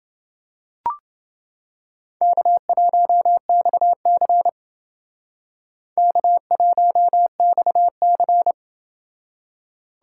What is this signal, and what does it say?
Morse code, heard as a keyed tone, about 28 wpm, 700 Hz. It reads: K1XC K1XC